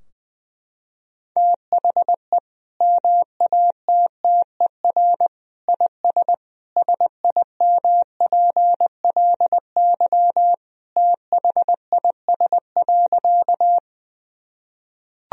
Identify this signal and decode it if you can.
Morse code, heard as a keyed tone, about 20 wpm, 700 Hz. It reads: THE MATTER IS SIMPLY THIS.
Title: Sentences from Top 1000 Words - Review Part 4 - 20wpm